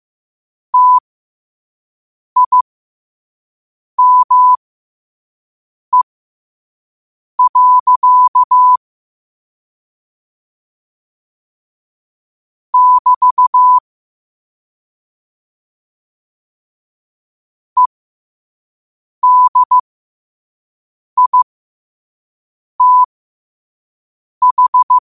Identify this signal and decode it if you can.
Morse code, heard as a keyed tone, about 15 wpm, spaced as for 5 wpm, 1000 Hz. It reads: TIME. <BT> EDITH
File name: qotdfiles / 5wpm